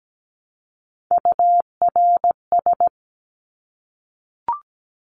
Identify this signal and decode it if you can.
Morse code, heard as a keyed tone, about 17 wpm, 700 Hz. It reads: URS
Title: Abbreviations 17wpm